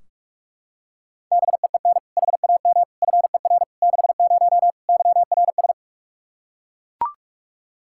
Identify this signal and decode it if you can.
Morse code, heard as a keyed tone, about 41 wpm, 700 Hz. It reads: BEEN HAM FER 60 YRS
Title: Practice QSO - Exchange 3 - More Info - 45wpm